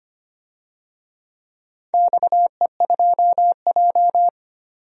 Morse code, heard as a keyed tone, about 25 wpm, 700 Hz. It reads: XE2J
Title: DXCC-100 Call Signs - 4 Characters - 25wpm